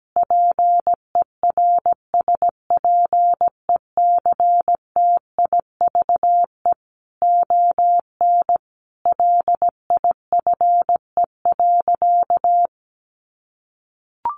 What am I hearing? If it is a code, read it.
Morse code, heard as a keyed tone, about 17 wpm, 700 Hz. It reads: PERSPECTIVE ON LIFE.
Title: Sentences from Top 1200 Words 17wpm